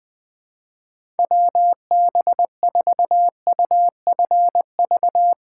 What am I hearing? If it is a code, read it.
Morse code, heard as a keyed tone, about 21 wpm, 700 Hz. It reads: WB4UFV
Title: USA Callsign DE Callsign - 6 Characters - 20wpm